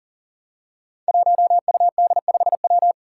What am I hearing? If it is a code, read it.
Morse code, heard as a keyed tone, about 42 wpm, 700 Hz. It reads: 1UD5W
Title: Five Challenging Letter-Number Combinations 40wpm